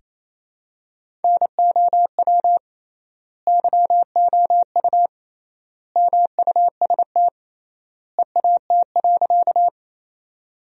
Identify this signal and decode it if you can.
Morse code, heard as a keyed tone, about 28 wpm, 700 Hz. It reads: NOW YOU MUST EAT.
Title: 3x Word-Spacing - Sentences from Top 300 Words - 28wpm